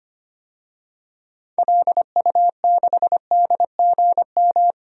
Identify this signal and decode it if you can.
Morse code, heard as a keyed tone, about 25 wpm, 700 Hz. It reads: LU6DGM
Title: DXCC-100 Call Signs - 6 Characters - 25wpm